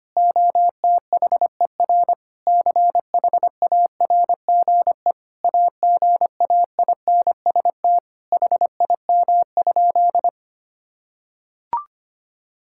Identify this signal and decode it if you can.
Morse code, heard as a keyed tone, about 25 wpm, 700 Hz. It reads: OTHER CHARGE AGAINST HIM?
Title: Sentences from Top 1000 Words - Review Part 6 - 25wpm